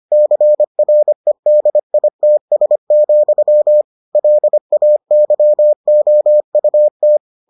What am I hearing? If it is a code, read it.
Morse code, heard as a keyed tone, about 25 wpm, 600 Hz. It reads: CREDITS, LAYOUT